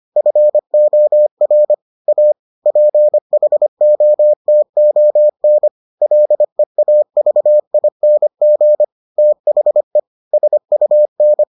Morse code, heard as a keyed tone, about 25 wpm, 600 Hz. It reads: FOR A PHOTON LEAVING THE SUN